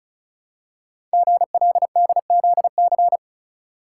Morse code, heard as a keyed tone, about 36 wpm, 690 Hz. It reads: GLDZC
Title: Code Groups - 5 Letters - 35wpm